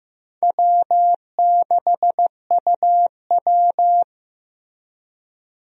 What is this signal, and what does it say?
Morse code, heard as a keyed tone, about 15 wpm, 700 Hz. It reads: W6UW
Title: USA Call Signs - 4 Characters - Repeated 2x - 15wpm